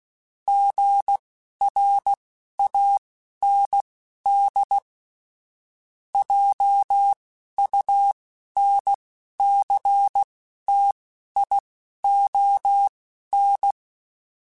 Morse code, heard as a keyed tone, about 16 wpm, spaced as for 11 wpm, 800 Hz. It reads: GRAND JUNCTION
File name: fromSMs / FromSMs mp3s / Cities@10